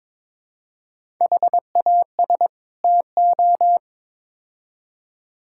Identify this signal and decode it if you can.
Morse code, heard as a keyed tone, about 22 wpm, 700 Hz. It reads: HAS TO
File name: practice / 22wpm / Build A Phrase - 5-Word Phrases - 22wpm